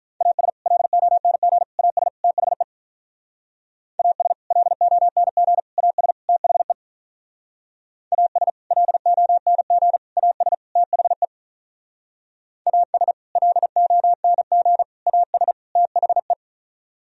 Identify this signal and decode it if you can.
Morse code, heard as a keyed tone, about 43 wpm, 700 Hz. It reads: AS LONG AS THE AS LONG AS THE AS LONG AS THE AS LONG AS THE
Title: Speed-Racing - Sets of 4 Words 35wpm